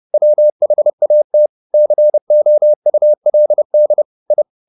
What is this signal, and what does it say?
Morse code, heard as a keyed tone, about 30 wpm, 600 Hz. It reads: WHAT COULD I